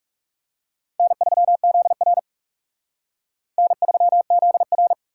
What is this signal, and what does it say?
Morse code, heard as a keyed tone, about 38 wpm, 700 Hz. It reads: N3ZR N3ZR